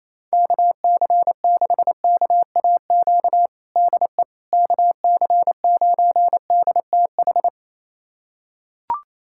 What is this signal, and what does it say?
Morse code, heard as a keyed tone, about 27 wpm, 700 Hz. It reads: KC6KAQ DE KC9DTH
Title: USA Callsign DE Callsign - 6 Characters - 28wpm